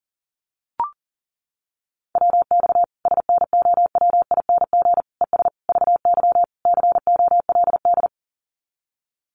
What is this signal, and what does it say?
Morse code, heard as a keyed tone, about 40 wpm, 700 Hz. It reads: WX SNOWING ES VY COLD
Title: Common QSO Phrases 40wpm